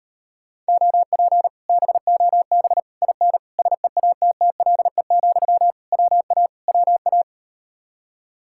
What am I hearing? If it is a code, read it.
Morse code, heard as a keyed tone, about 38 wpm, 700 Hz. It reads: OP BOB IN SEATTLE, WA WA